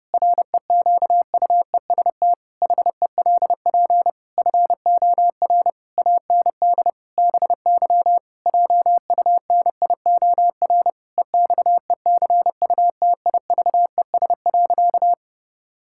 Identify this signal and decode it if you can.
Morse code, heard as a keyed tone, about 32 wpm, 700 Hz. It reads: REQUEST HELP FOR AND BY JUNIOR EXECUTIVES.
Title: Pangram Challenge v2 - 30wpm